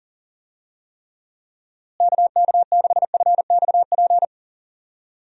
Morse code, heard as a keyed tone, about 40 wpm, 700 Hz. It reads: KK6FXP
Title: USA Call Signs - 6 Characters - Repeated 2x - 40wpm